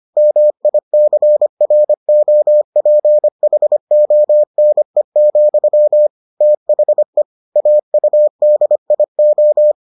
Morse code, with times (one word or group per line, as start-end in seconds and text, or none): 0.2-6.1 MICROPHONE,
6.4-7.2 THE
7.6-9.7 AUDIO